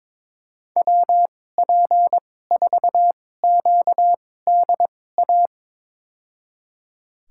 Morse code, heard as a keyed tone, about 22 wpm, spaced as for 16 wpm, 700 Hz. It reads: WP4QDA
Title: USA Call Signs - 6 Characters - 2x Character Spacing - 22wpm